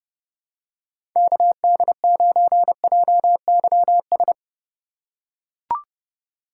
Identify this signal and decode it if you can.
Morse code, heard as a keyed tone, about 31 wpm, 700 Hz. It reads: KD9JYS
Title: USA Call Signs - 6 Characters - 3x Character Spacing - 30wpm